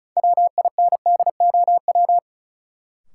Morse code, heard as a keyed tone, about 36 wpm, 700 Hz. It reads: WINDOW
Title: Sentences from Top 1600 Words - New Word Drill Alt - 35wpm